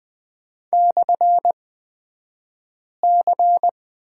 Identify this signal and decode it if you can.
Morse code, heard as a keyed tone, about 20 wpm, spaced as for 5 wpm, 700 Hz. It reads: /C
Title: 235 Lesson 38d - Letter q - Two Characters 20wpm